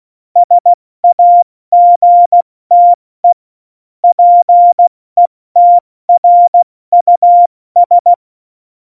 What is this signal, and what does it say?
Morse code, heard as a keyed tone, about 16 wpm, 700 Hz. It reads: SAGTE PETRUS
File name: Morsekurs / Witze / Witz 26. Januar 2021, 70 F80